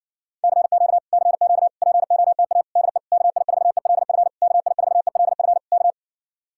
Morse code, heard as a keyed tone, about 62 wpm, 700 Hz. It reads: CQ CQ POTA DE KI4ELF KI4ELF K
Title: Speed-Racing Common QSO Phrases Encore - 50wpm